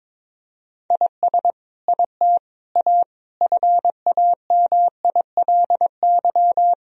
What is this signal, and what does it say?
Morse code, heard as a keyed tone, about 22 wpm, 700 Hz. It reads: IS IT A FAMILY
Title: Sentences from Top 1000 Words - Review Part 3 - 22wpm